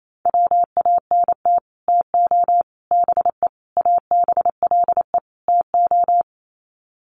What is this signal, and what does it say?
Morse code, heard as a keyed tone, about 28 wpm, 700 Hz. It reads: WANT TO BE ABLE TO
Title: Sets of 5 Words 28wpm